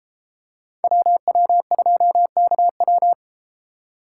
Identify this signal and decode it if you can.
Morse code, heard as a keyed tone, about 32 wpm, 700 Hz. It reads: WW2KW